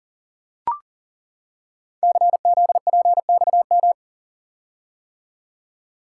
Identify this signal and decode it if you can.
Morse code, heard as a keyed tone, about 38 wpm, 700 Hz. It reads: CZPXM
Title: Code Groups - 5 Letters - 40wpm